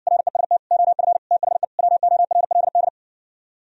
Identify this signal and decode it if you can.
Morse code, heard as a keyed tone, about 59 wpm, 700 Hz. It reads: REST OF THE WORLD